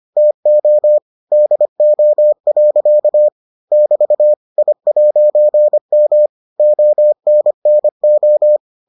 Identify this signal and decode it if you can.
Morse code, heard as a keyed tone, about 25 wpm, 600 Hz. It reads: TO DO. <BT> I'M ONNO